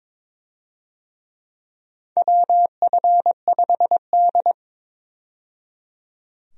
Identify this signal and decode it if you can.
Morse code, heard as a keyed tone, about 22 wpm, 700 Hz. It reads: WF5D
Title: USA Call Signs - 4 Characters - 22wpm